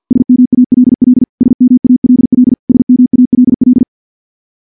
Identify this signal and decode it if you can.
Morse code, heard as a keyed tone, about 49 wpm, 250 Hz. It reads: SMALL SMALL SMALL
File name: morse / morse-training-50WPM